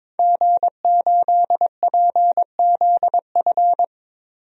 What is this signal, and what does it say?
Morse code, heard as a keyed tone, about 22 wpm, 700 Hz. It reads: G8PZF
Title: DXCC-100 Call Signs - 5 Characters - 22wpm